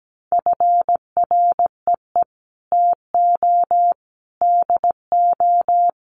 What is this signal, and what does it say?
Morse code, heard as a keyed tone, about 17 wpm, 700 Hz. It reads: FREE TO DO
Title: Sentences from Top 500 Words 17wpm